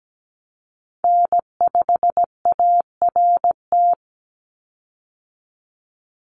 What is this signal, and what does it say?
Morse code, heard as a keyed tone, about 17 wpm, 700 Hz. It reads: N5ART
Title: USA Call Signs 17wpm